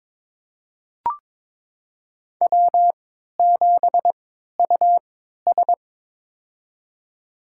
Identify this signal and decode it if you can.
Morse code, heard as a keyed tone, about 21 wpm, spaced as for 12 wpm, 700 Hz. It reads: W7US
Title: USA Call Signs - 4 Characters - 3x Character Spacing - 22wpm